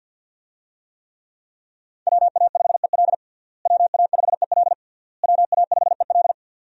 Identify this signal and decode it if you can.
Morse code, heard as a keyed tone, about 50 wpm, 700 Hz. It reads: WA5EL WA5EL WA5EL
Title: USA Call Signs - 5 Characters - Repeated 3x - 50wpm